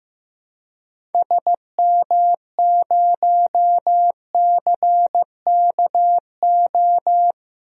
Morse code, heard as a keyed tone, about 16 wpm, 700 Hz. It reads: SM0CKO
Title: DXCC-100 Call Signs - 6 Characters - 15wpm